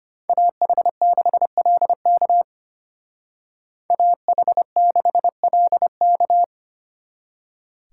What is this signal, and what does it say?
Morse code, heard as a keyed tone, about 33 wpm, 700 Hz. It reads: AH6LK AH6LK